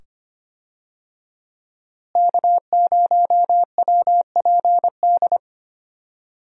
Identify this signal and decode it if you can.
Morse code, heard as a keyed tone, about 25 wpm, 700 Hz. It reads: K0WPD
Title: USA Call Signs - 5 Characters - Repeated 2x - 25wpm